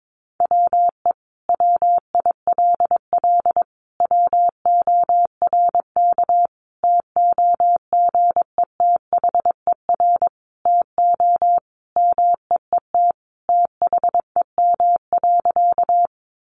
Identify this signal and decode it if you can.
Morse code, heard as a keyed tone, about 22 wpm, 700 Hz. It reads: WE WILL WORK TOGETHER TO MEET THEM.
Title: Sentences from Top 1100 Words 22wpm